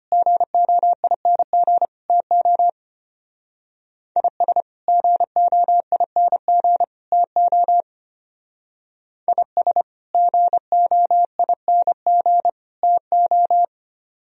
Morse code, heard as a keyed tone, about 31 wpm, 700 Hz. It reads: GOING TO IS GOING TO IS GOING TO